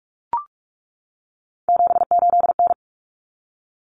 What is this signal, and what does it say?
Morse code, heard as a keyed tone, about 44 wpm, 700 Hz. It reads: 78N
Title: Common Cut Numbers - 3-Digit 45wpm